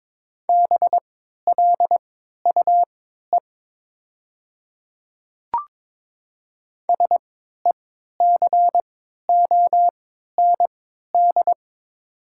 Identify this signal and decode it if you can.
Morse code, heard as a keyed tone, about 22 wpm, spaced as for 12 wpm, 700 Hz. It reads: BLUE SECOND
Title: 3x Character Spacing - Top 500 Words - 22wpm